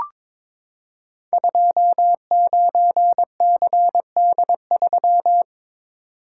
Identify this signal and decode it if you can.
Morse code, heard as a keyed tone, about 22 wpm, 700 Hz. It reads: 29CD3
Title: Code Groups - 5 Letter:Numbers - 22wpm